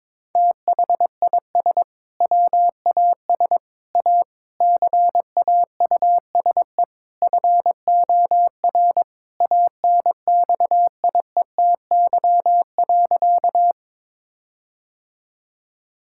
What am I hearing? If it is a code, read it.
Morse code, heard as a keyed tone, about 21 wpm, 700 Hz. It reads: THIS WAS A CAUSE FOR ANXIETY.
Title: Sentences from Top 1600 Words 22wpm